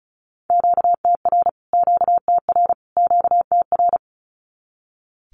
Morse code, heard as a keyed tone, about 34 wpm, 700 Hz. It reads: QTR QTR QTR
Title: Q-Codes Repeated 3x 35wpm